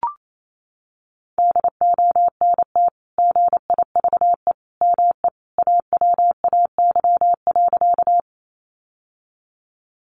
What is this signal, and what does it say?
Morse code, heard as a keyed tone, about 28 wpm, 700 Hz. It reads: DONT GIVE ME AWAY.
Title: Sentences from Top 600 Words 28wpm